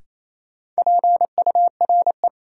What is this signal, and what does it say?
Morse code, heard as a keyed tone, about 28 wpm, 700 Hz. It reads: PURE
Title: Sentences from Top 1500 Words - New Word Drill - 28wpm